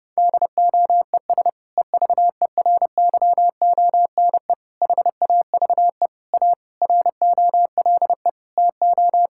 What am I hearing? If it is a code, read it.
Morse code, heard as a keyed tone, about 30 wpm, 700 Hz. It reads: DOES EVERYONE HAVE A ROLE TO
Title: Sentences from Top 1000 Words - Review Part 6 - 30wpm